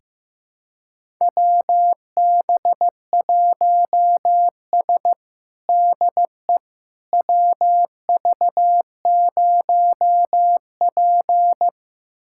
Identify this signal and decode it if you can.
Morse code, heard as a keyed tone, about 15 wpm, 700 Hz. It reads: WB1S DE WV0P
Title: USA Callsign DE Callsign - 4 Characters - 15wpm